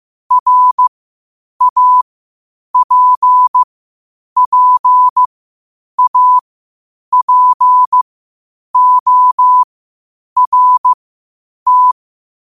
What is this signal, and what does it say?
Morse code, heard as a keyed tone, about 15 wpm, spaced as for 8 wpm, 1000 Hz. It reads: RAPPAPORT
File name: qotdfiles / 8wpm